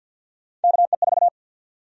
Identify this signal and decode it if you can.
Morse code, heard as a keyed tone, about 52 wpm, 700 Hz. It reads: KE4
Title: USA Call Sign Prefixes - 50wpm